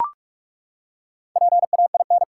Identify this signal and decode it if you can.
Morse code, heard as a keyed tone, about 42 wpm, 700 Hz. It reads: PAIN
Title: Four Letter Words - 45wpm